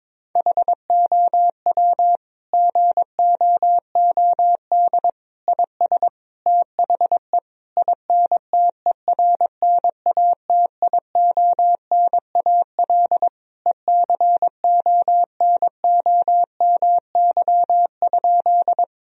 Morse code, heard as a keyed tone, about 22 wpm, 700 Hz. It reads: HOW GOOD IS THE INTERNATIONAL ECONOMY?
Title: Sentences from Top 1000 Words - Review Part 4 - 22wpm